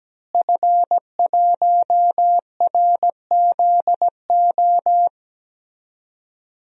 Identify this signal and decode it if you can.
Morse code, heard as a keyed tone, about 17 wpm, 700 Hz. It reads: F1RZO
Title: DXCC-100 Call Signs - 5 Characters - Repeated 2x - 17wpm